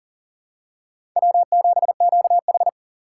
Morse code, heard as a keyed tone, about 40 wpm, 700 Hz. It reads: W7QH
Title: USA Call Signs - 4 Characters - 3x Character Spacing - 40wpm